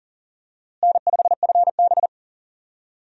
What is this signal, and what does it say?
Morse code, heard as a keyed tone, about 40 wpm, 700 Hz. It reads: N5FB
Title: USA Call Signs - 4 Characters - Repeated 2x - 40wpm